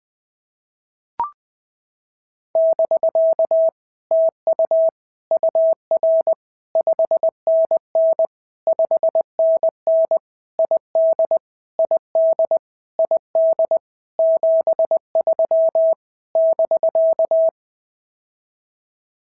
Morse code, heard as a keyed tone, about 20 wpm, 650 Hz. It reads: <BK> TU UR 5NN 5NN ID ID ID 73 <BK>
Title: POTA QSO Practice - 20wpm